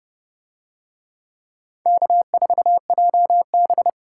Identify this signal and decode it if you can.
Morse code, heard as a keyed tone, about 29 wpm, 700 Hz. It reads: K4JB